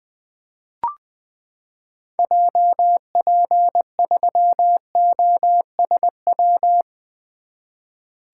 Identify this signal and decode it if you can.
Morse code, heard as a keyed tone, about 20 wpm, 700 Hz. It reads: JP3OSW
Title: 2023 SOTA Activator Call Signs - 20wpm